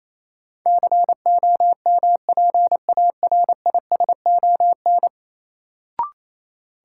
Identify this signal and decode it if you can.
Morse code, heard as a keyed tone, about 28 wpm, 700 Hz. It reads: COMPARISON
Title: Sentences from Top 1600 Words - New Word Drill - 28wpm